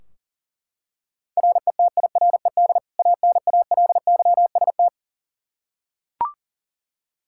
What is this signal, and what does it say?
Morse code, heard as a keyed tone, about 40 wpm, 700 Hz. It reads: RETIRED ANALYST
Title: Practice QSO - Exchange 3 - More Info - 40wpm